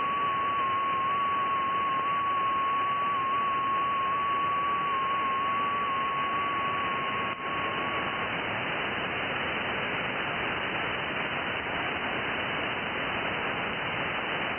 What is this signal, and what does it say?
FT8.
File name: IC-7100